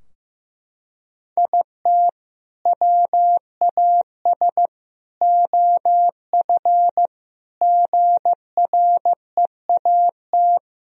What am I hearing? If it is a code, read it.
Morse code, heard as a keyed tone, about 15 wpm, 700 Hz. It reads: IT WAS OF GREAT
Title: Sentences from Top 1000 Words - Review Part 2 - 15wpm